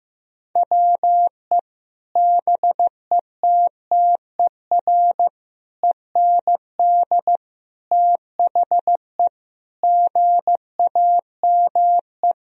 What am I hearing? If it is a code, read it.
Morse code, heard as a keyed tone, about 15 wpm, 700 Hz. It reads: WE BETTER END THE GAME
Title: Sentences from Top 1000 Words - Review Part 3 - 15wpm